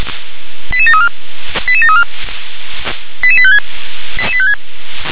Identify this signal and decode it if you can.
Five-tone selective call (ZVEI), heard as quick runs of five tones, tones of about 80 ms. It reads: 88734 88734 88856